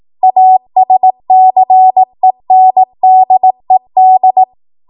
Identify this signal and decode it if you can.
Morse code, heard as a keyed tone, about 18 wpm, 750 Hz. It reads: ASCENDED